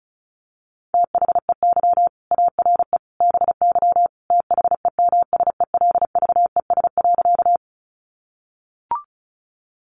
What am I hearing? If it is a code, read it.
Morse code, heard as a keyed tone, about 34 wpm, 700 Hz. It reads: THEY ARE BY THEMSELVES.
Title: Sentences from Top 800 Words 35wpm